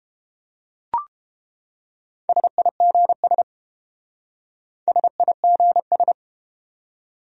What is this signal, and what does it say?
Morse code, heard as a keyed tone, about 30 wpm, 700 Hz. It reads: SIGS SIGS